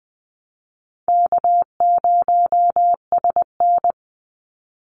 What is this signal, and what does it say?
Morse code, heard as a keyed tone, about 20 wpm, 700 Hz. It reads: K0SN
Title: Call Signs Repeated 3x - 20wpm